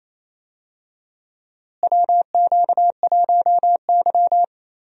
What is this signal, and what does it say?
Morse code, heard as a keyed tone, about 28 wpm, 700 Hz. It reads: WQ1Y
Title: USA Call Signs - 4 Characters - Repeated 2x - 28wpm